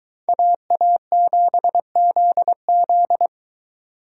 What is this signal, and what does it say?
Morse code, heard as a keyed tone, about 22 wpm, 700 Hz. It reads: AA7ZZ